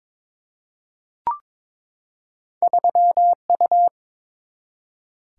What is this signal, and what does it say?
Morse code, heard as a keyed tone, about 22 wpm, 690 Hz. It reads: 3U
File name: practice / 22wpm / Two Letter-Number Combinations - 22wpm